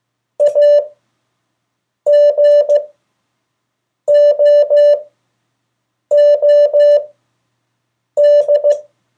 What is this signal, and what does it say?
Morse code, heard as a keyed tone, about 15 wpm, spaced as for 6 wpm, 580 Hz. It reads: A GOOD
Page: CW Class Review – Delaware Valley Radio Association